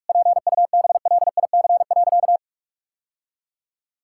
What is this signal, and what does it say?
Morse code, heard as a keyed tone, about 42 wpm, 700 Hz. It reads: PUBLIC.
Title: Sentences from Top 1000 Words - Review Part 5 - 45wpm